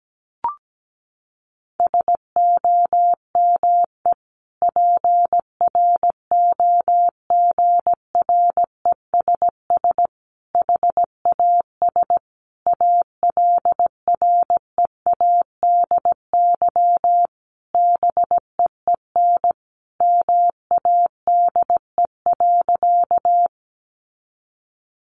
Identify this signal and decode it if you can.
Morse code, heard as a keyed tone, about 17 wpm, 700 Hz. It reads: SOME PROGRESS HAS ALREADY BEEN MADE.
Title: Sentences from Top 1400 Words 17wpm